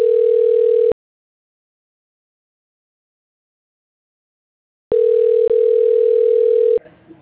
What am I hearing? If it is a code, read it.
Ringback tone (North America), 440 + 480 Hz.